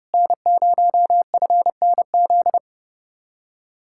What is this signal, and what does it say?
Morse code, heard as a keyed tone, about 38 wpm, 700 Hz. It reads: N0FNZ